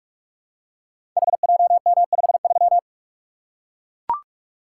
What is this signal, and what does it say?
Morse code, heard as a keyed tone, about 44 wpm, 700 Hz. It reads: HJM53
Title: Code Groups - 5 Letter:Numbers - 45wpm